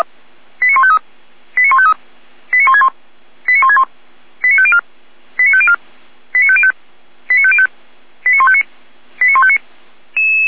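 Five-tone selective call (ZVEI), heard as quick runs of five tones, tones of about 70 ms. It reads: 89153 89153 89161 89161 89504 89504 89505 89505 89159 89159
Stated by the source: Received on SDR.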